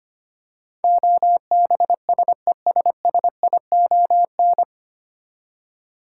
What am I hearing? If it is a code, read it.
Morse code, heard as a keyed tone, about 25 wpm, 700 Hz. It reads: OBSESSION
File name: practice / 25wpm / 9 Letter Words - First 3 Letters Given - 25wpm